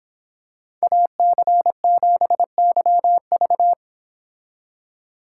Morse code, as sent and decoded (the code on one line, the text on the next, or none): .- -.-. --... -.-- ...-
AC7YV